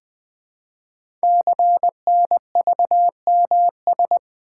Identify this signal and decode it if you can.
Morse code, heard as a keyed tone, about 20 wpm, 700 Hz. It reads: CNVMS
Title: Code Groups - 5 Letters - 20wpm